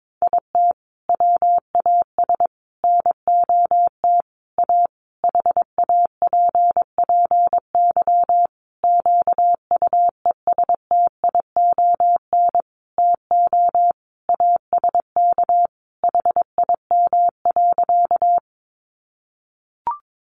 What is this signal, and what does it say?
Morse code, heard as a keyed tone, about 22 wpm, 700 Hz. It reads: IT WAS NOT A HAPPY QUESTION TO ASK HIM.
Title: Sentences from Top 1000 Words 22wpm